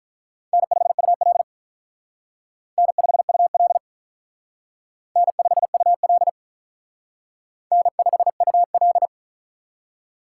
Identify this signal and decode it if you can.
Morse code, heard as a keyed tone, about 44 wpm, 700 Hz. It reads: N5UL N5UL N5UL N5UL
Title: Speed Racing - USA 1x2 Call Signs 35wpm